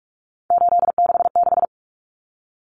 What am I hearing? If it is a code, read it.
Morse code, heard as a keyed tone, about 44 wpm, 700 Hz. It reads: 866
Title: Common Cut Numbers - 3-Digit 45wpm